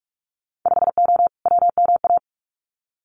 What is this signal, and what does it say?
Morse code, heard as a keyed tone, about 46 wpm, 700 Hz. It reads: 5O WMA
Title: Winter Field Day Exchanges Encore 45wpm